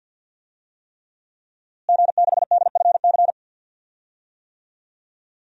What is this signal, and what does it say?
Morse code, heard as a keyed tone, about 46 wpm, 700 Hz. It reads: G6DFC
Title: DXCC-100 Call Signs - 5 Characters - Repeated 3x - 50wpm